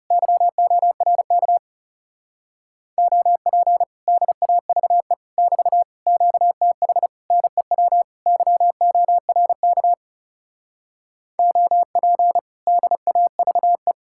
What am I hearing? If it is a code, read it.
Morse code, heard as a keyed tone, about 40 wpm, 700 Hz. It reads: YORK OP DAVE <BT> QTH NEW YORK OP DAVE